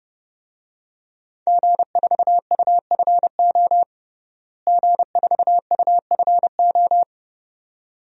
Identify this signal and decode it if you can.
Morse code, heard as a keyed tone, about 30 wpm, 700 Hz. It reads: G4UFO G4UFO
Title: DXCC-100 Call Signs - 5 Characters - Repeated 2x - 30wpm